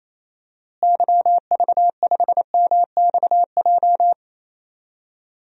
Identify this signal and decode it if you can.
Morse code, heard as a keyed tone, about 28 wpm, 700 Hz. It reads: YV5MXJ
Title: DXCC-100 Call Signs - 6 Characters - Repeated 2x - 28wpm